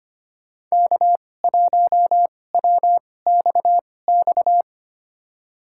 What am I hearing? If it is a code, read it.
Morse code, heard as a keyed tone, about 25 wpm, spaced as for 18 wpm, 700 Hz. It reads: K1WXX
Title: USA Call Signs - 5 Characters - 2x Character Spacing - 25wpm